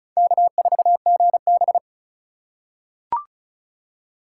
Morse code, heard as a keyed tone, about 36 wpm, 700 Hz. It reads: K4GB
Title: USA Call Signs - 4 Characters - 2x Character Spacing - 35wpm